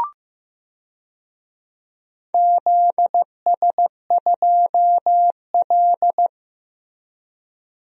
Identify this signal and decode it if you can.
Morse code, heard as a keyed tone, about 15 wpm, 700 Hz. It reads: ZS2L
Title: DXCC-100 Call Signs - 4 Characters - Repeated 3x - 15wpm